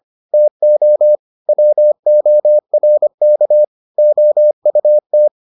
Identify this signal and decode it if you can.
Morse code, heard as a keyed tone, about 25 wpm, 600 Hz. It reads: TO WORK OUT